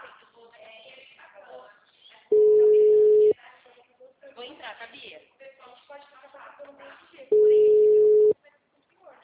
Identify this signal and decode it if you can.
Ringback tone (Europe, 425 Hz), with speech.